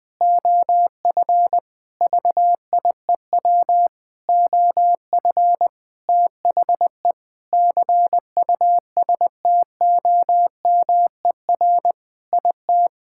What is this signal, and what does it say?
Morse code, heard as a keyed tone, about 20 wpm, 700 Hz. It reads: OF VIEW OF THE CUSTOMER IT
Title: Sentences with 7-Word Phrases - 20wpm